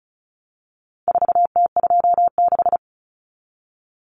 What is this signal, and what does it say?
Morse code, heard as a keyed tone, about 34 wpm, 700 Hz. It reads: 4T26
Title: Common Cut Numbers - 4-Digit 35wpm